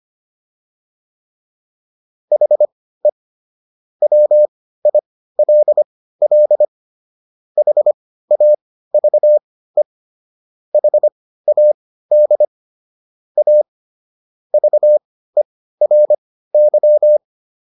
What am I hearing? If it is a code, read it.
Morse code, heard as a keyed tone, about 25 wpm, spaced as for 15 wpm, 600 Hz. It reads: HE WILL HAVE HAD A VERY